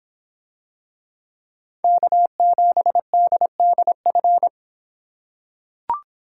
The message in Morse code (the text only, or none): K7DDF